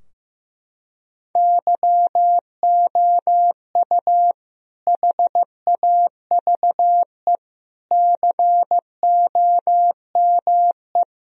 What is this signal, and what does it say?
Morse code, heard as a keyed tone, about 15 wpm, 700 Hz. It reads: YOU HAVE COME
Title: Sentences from Top 1000 Words - Review Part 2 - 15wpm